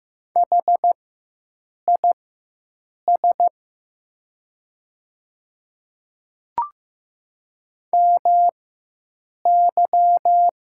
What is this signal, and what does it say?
Morse code, heard as a keyed tone, about 15 wpm, spaced as for 7 wpm, 700 Hz. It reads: HIS MY
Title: 4x Character Spacing - Top 100 Words - 15wpm